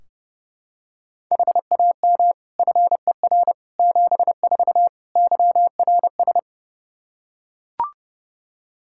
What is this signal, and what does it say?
Morse code, heard as a keyed tone, about 31 wpm, 700 Hz. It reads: HAM FER 74 YRS
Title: Practice QSO - Exchange 3 - More Info - 30wpm